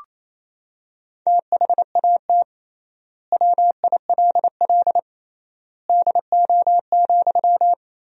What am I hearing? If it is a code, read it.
Morse code, heard as a keyed tone, about 28 wpm, 700 Hz. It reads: THAT WILL DO,
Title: 3x Word-Spacing - Sentences from Top 100 Words - 28wpm